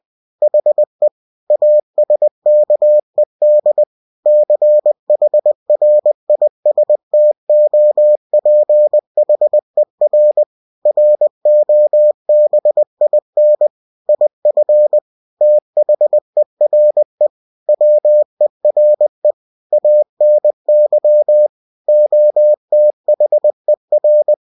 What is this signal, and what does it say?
Morse code, heard as a keyed tone, about 20 wpm, 600 Hz. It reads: HE ASKED CHRISTOPHER ROBIN IF THERE WERE ANY OTHER
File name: cw-books / winnie-the-pooh / winnie-20wpm-page-0093